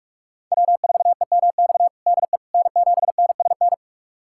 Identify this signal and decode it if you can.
Morse code, heard as a keyed tone, about 47 wpm, 700 Hz. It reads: W4EMX DE N7NSN
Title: USA Callsign DE Callsign - 5 Characters - 45wpm